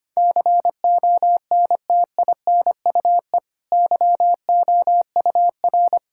Morse code, heard as a keyed tone, about 25 wpm, 700 Hz. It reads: CONTINUE YOUR
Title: Sentences from Top 1000 Words - Review Part 6 - 25wpm